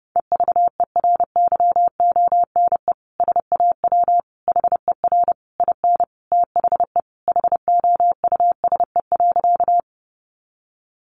Morse code, heard as a keyed tone, about 30 wpm, 700 Hz. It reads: EVERYONE SAW HER IN THE HOUSE.
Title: Sentences from Top 800 Words 30wpm